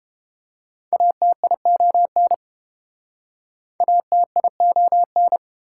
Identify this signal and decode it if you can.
Morse code, heard as a keyed tone, about 29 wpm, 700 Hz. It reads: ATION ATION